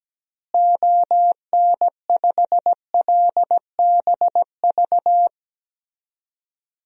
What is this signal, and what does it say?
Morse code, heard as a keyed tone, about 17 wpm, 700 Hz. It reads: ON5LBV
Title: DXCC-100 Call Signs - 6 Characters - Repeated 2x - 17wpm